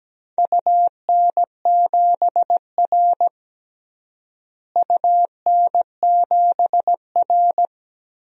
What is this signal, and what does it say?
Morse code, heard as a keyed tone, about 17 wpm, 700 Hz. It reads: UN7R UN7R
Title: DXCC-100 Call Signs - 4 Characters - Repeated 2x - 17wpm